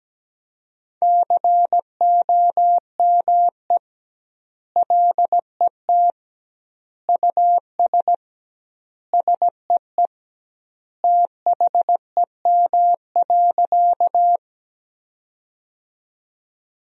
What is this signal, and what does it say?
Morse code, heard as a keyed tone, about 17 wpm, 700 Hz. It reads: COME LET US SEE THEM.
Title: 2x Word-Spacing - Sentences from Top 300 Words - 17wpm